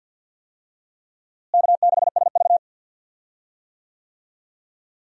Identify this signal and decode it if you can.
Morse code, heard as a keyed tone, about 53 wpm, 700 Hz. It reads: K6SV
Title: USA Call Signs - 4 Characters - Repeated 2x - 50wpm